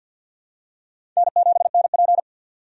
Morse code, heard as a keyed tone, about 51 wpm, 700 Hz. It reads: N7NP
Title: USA Call Signs - 4 Characters - 50wpm